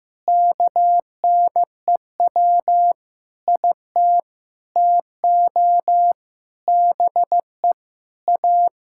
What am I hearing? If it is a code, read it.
Morse code, heard as a keyed tone, about 15 wpm, 700 Hz. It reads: KNEW IT TO BE A
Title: Sentences from Top 1600 Words 15wpm